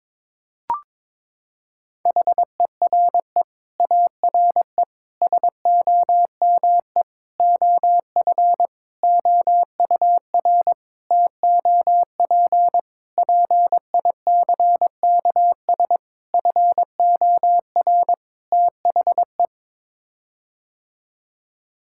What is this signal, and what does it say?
Morse code, heard as a keyed tone, about 22 wpm, 700 Hz. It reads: HERE ARE SOME OF OUR TOP PICKS FOR THE
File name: Sets of 9 Words - 22wpm